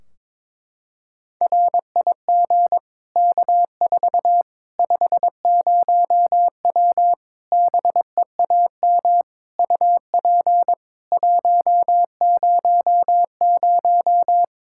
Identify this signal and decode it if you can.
Morse code, heard as a keyed tone, about 22 wpm, 700 Hz. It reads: RIG K4 50W BEAM UP 100
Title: Practice QSO - Exchange 2 - Info - 22wpm